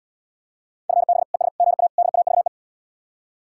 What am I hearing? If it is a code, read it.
Morse code, heard as a keyed tone, about 77 wpm, 700 Hz. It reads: FB ES GA DAVE